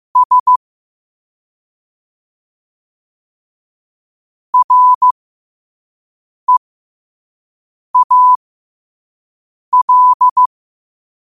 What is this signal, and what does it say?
Morse code, heard as a keyed tone, about 15 wpm, spaced as for 5 wpm, 1000 Hz. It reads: S REAL
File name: qotdfiles / 5wpm